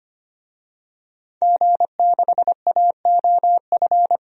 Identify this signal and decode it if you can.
Morse code, heard as a keyed tone, about 25 wpm, 700 Hz. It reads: G6AOF
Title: DXCC-100 Call Signs - 5 Characters - 25wpm